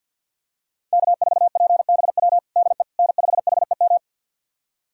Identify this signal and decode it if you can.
Morse code, heard as a keyed tone, about 53 wpm, 700 Hz. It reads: K4PBW DE N5HEM
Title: USA Callsign DE Callsign - 5 Characters - 50wpm